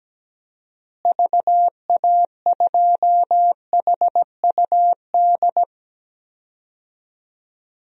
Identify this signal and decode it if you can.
Morse code, heard as a keyed tone, about 17 wpm, 700 Hz. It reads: VA2HUD
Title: DXCC-100 Call Signs - 6 Characters - Repeated 2x - 17wpm